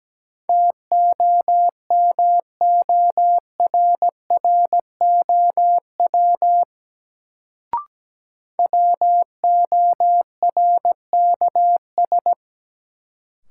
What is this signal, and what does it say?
Morse code, heard as a keyed tone, about 17 wpm, 700 Hz. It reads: TOMORROW WORKS
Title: Sentences from Top 1100 Words - New Word Drill Alt - 17wpm